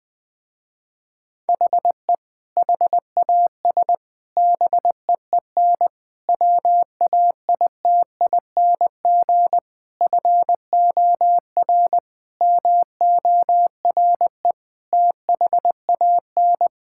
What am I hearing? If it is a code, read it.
Morse code, heard as a keyed tone, about 20 wpm, 700 Hz. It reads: HE HAS BEEN WAITING FOR MORE THAN
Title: Sentences with 3-Word Phrases - Spoken Beginning - 20wpm